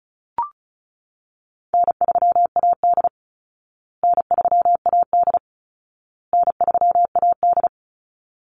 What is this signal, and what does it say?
Morse code, heard as a keyed tone, about 34 wpm, 700 Hz. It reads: N3AD N3AD N3AD
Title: Call Signs Repeated 2x - 35wpm